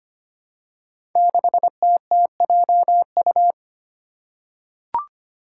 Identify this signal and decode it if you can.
Morse code, heard as a keyed tone, about 25 wpm, 700 Hz. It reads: 6TTJU
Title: Five Challenging Letter-Number Combinations 25wpm